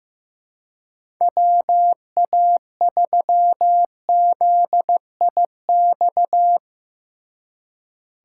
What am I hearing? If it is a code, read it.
Morse code, heard as a keyed tone, about 16 wpm, 700 Hz. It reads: WA3ZIX
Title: Speed Racing - USA Call Signs 15wpm